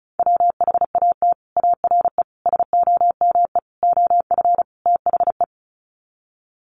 Morse code, heard as a keyed tone, about 34 wpm, 700 Hz. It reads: WHAT ARE SOME OF THE
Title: Sets of 5 Words Encore 35wpm